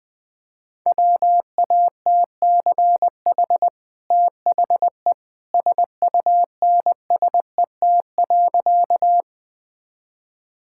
Morse code, heard as a keyed tone, about 20 wpm, 700 Hz. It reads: WATCH THE SUNSET.
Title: Easy Sentences with Hints - 20wpm